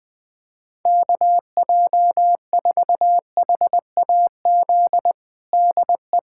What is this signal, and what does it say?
Morse code, heard as a keyed tone, about 21 wpm, 700 Hz. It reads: KJ4HAZ DE
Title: USA Callsign DE Callsign - 6 Characters - 20wpm